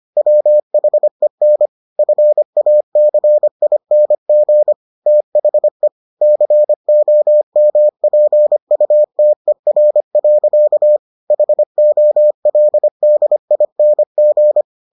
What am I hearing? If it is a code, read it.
Morse code, heard as a keyed tone, about 25 wpm, 600 Hz. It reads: WHEN FACING THE COMPUTER. HOLDING